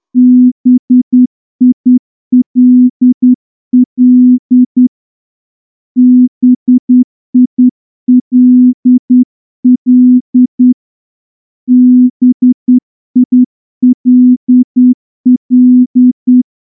Morse code, heard as a keyed tone, about 10 wpm, 250 Hz. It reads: BILL BILL BILL
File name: morse-training-10WPM